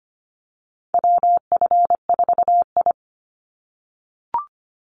Morse code, heard as a keyed tone, about 25 wpm, 700 Hz. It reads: WF4I
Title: NA SOTA Chaser Callsigns 25wpm